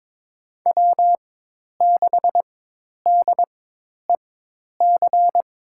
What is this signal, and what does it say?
Morse code, heard as a keyed tone, about 22 wpm, spaced as for 10 wpm, 700 Hz. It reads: W6DEC
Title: USA Call Signs - 5 Characters - 4x Character Spacing - 22wpm